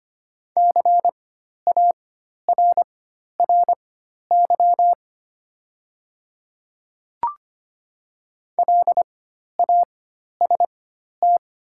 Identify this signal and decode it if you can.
Morse code, heard as a keyed tone, about 25 wpm, spaced as for 12 wpm, 700 Hz. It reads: CARRY LAST
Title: 4x Character Spacing - Top 300 Words - 25wpm